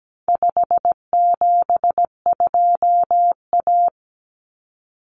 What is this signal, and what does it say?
Morse code, heard as a keyed tone, about 17 wpm, 700 Hz. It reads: 572A
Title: Common Cut Numbers - 4-Digit 17wpm